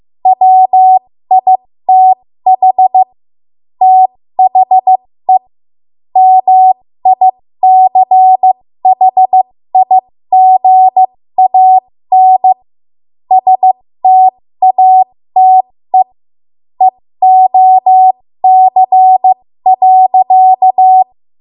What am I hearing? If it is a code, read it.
Morse code, heard as a keyed tone, about 15 wpm, spaced as for 13 wpm, 750 Hz. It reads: WITH THE MICHIGAN STATE EOC.
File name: Morse / 13WPM